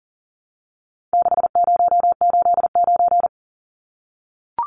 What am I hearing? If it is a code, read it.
Morse code, heard as a keyed tone, about 40 wpm, 700 Hz. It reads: 6089
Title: Numbers - 4-Digit 40wpm